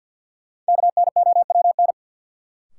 Morse code, heard as a keyed tone, about 52 wpm, 700 Hz. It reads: KNOWN
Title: Sentences from Top 600 Words - New Word Drill - 50wpm